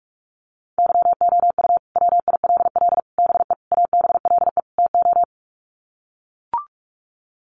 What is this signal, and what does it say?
Morse code, heard as a keyed tone, about 44 wpm, 700 Hz. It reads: YOU WILL BE ABLE TO
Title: Sets of 5 Words Encore2 45wpm